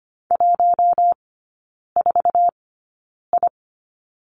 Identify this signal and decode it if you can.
Morse code, heard as a keyed tone, about 25 wpm, spaced as for 9 wpm, 700 Hz. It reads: 14I
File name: Lesson 17e - Letter u - Three Characters 25wpm